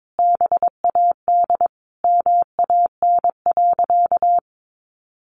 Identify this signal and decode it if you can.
Morse code, heard as a keyed tone, about 22 wpm, 700 Hz. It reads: BAD MAN.
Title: Sentences from Top 1200 Words 22wpm